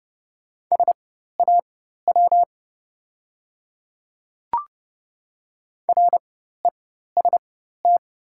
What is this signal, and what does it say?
Morse code, heard as a keyed tone, about 30 wpm, spaced as for 14 wpm, 700 Hz. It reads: SAW REST
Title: 4x Character Spacing - Top 500 Words - 30wpm